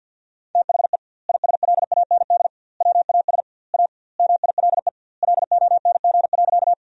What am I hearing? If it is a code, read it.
Morse code, heard as a keyed tone, about 48 wpm, 700 Hz. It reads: THE ISLAND WAS A MILE LONG.
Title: 2x Word-Spacing - Sentences from Top 500 Words - 50wpm